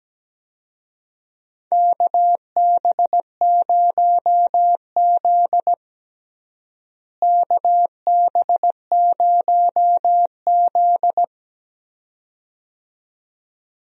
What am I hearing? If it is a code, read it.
Morse code, heard as a keyed tone, about 17 wpm, 700 Hz. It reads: KB0Z KB0Z
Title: USA Call Signs - 4 Characters - Repeated 2x - 17wpm